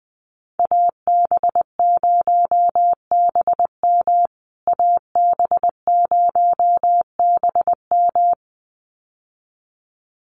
Morse code, heard as a keyed tone, about 20 wpm, 700 Hz. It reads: AB0BM AB0BM
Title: USA Call Signs Repeated 2x 20wpm